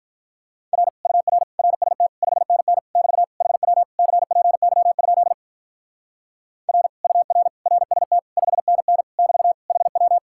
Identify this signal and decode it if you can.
Morse code, heard as a keyed tone, about 44 wpm, 700 Hz. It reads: R UR RST 5NN <BT> HW CPY? R UR RST 5NN <BT> HW